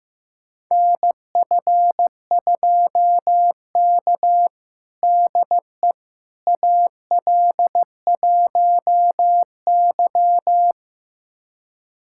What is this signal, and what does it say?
Morse code, heard as a keyed tone, about 15 wpm, 700 Hz. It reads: NF2K DE AL1Y
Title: USA Callsign DE Callsign - 4 Characters - 15wpm